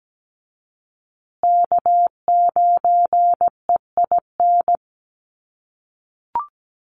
Morse code, heard as a keyed tone, about 17 wpm, 700 Hz. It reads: K9EIN
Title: USA Call Signs Repeated 2x 17wpm